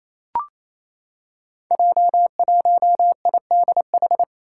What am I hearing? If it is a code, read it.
Morse code, heard as a keyed tone, about 28 wpm, 700 Hz. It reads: J1IDH
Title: Five Challenging Letter-Number Combinations 28wpm